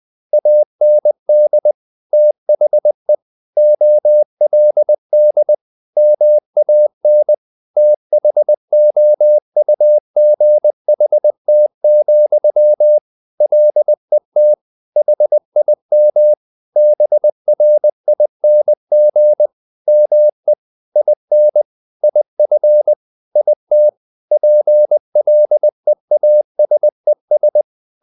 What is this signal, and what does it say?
Morse code, heard as a keyed tone, about 20 wpm, 600 Hz. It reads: AND THE OLD MAN THOUGHT, LET HIM BRING ME IN IF IT PLEASES
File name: the old man and the sea part 2 20wpm